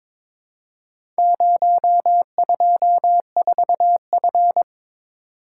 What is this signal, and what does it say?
Morse code, heard as a keyed tone, about 22 wpm, 700 Hz. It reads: 024F
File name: practice / 22wpm / Four Letter-Number Combinations - 22wpm